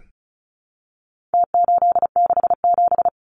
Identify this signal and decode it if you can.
Morse code, heard as a keyed tone, about 34 wpm, 700 Hz. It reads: T867
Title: Numbers - Contest Style - 4-Digit 35wpm